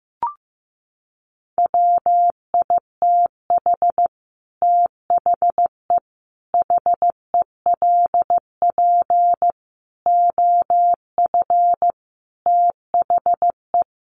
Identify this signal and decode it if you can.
Morse code, heard as a keyed tone, about 15 wpm, 700 Hz. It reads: WITH THE HELP OF THE
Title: Sets of 5 Words Encore 15wpm